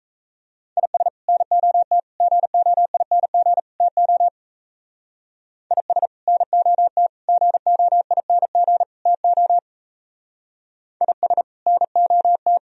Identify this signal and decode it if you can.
Morse code, heard as a keyed tone, about 34 wpm, 700 Hz. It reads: IS NOT GOING TO IS NOT GOING TO IS NOT
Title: Speed-Racing - Sets of 4 Words 28wpm